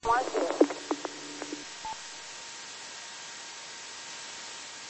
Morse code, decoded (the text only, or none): none